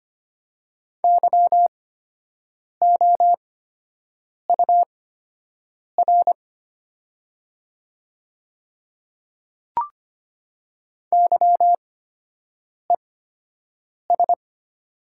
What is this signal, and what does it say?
Morse code, heard as a keyed tone, about 25 wpm, spaced as for 7 wpm, 700 Hz. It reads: YOUR YES
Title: 8x Character Spacing - Top 500 Words - 25wpm